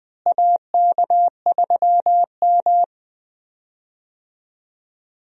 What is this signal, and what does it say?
Morse code, heard as a keyed tone, about 20 wpm, 700 Hz. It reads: AK3M
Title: USA Call Signs - 4 Characters - Repeated 2x - 20wpm